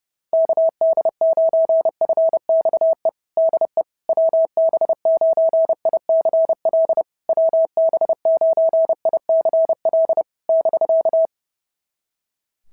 Morse code, heard as a keyed tone, about 30 wpm, 690 Hz. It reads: KD9FXE DE WB9ICL WB9ICL <BK>
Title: SOTA QSO Practice 30wpm